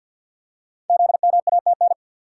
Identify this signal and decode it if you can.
Morse code, heard as a keyed tone, about 53 wpm, 700 Hz. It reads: ZMATN
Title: Code Groups - 5 Letters - 50wpm